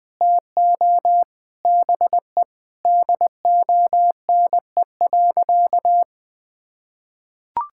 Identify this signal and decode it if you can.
Morse code, heard as a keyed tone, about 20 wpm, 700 Hz. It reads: TO BE DONE.
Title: Sentences with 9-Word Phrases - 20wpm